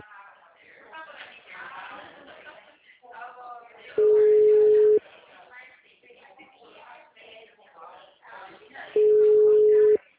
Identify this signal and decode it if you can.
Ringback tone (Europe, 425 Hz), with speech.
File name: out- -2009- - - .24230